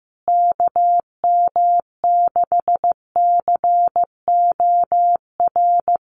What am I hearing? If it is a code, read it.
Morse code, heard as a keyed tone, about 15 wpm, 700 Hz. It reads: KM6COR
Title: USA Call Signs Repeated 3x 15wpm